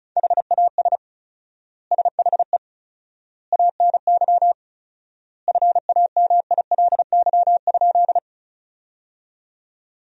Morse code, heard as a keyed tone, about 34 wpm, 700 Hz. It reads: HAS SHE ANY FAMILY?
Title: 4x Word-Spacing - Sentences from Top 400 Words - 35wpm